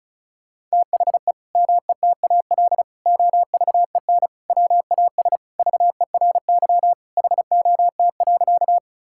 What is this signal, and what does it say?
Morse code, heard as a keyed tone, about 36 wpm, 700 Hz. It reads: THE METAL OVEN WAS VERY HOT.
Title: Sentences from Top 1000 Words - Review Part 6 - 35wpm